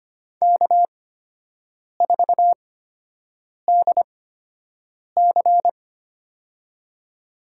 Morse code, heard as a keyed tone, about 25 wpm, spaced as for 7 wpm, 700 Hz. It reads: K4DC
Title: USA Call Signs - 8x Character Spacing - 25wpm